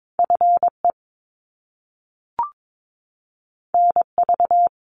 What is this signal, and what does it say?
Morse code, heard as a keyed tone, about 22 wpm, 700 Hz. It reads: FE NV
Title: Top Two Letter Combinations in Words 22wpm